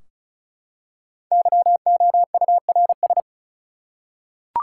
Morse code, heard as a keyed tone, about 34 wpm, 700 Hz. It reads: YOURS
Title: Five Letter Words - 35wpm